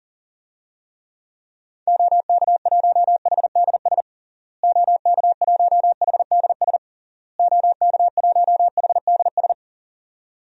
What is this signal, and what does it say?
Morse code, heard as a keyed tone, about 37 wpm, 700 Hz. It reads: OK1HDS OK1HDS OK1HDS
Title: DXCC-100 Call Signs - 6 Characters - Repeated 3x - 40wpm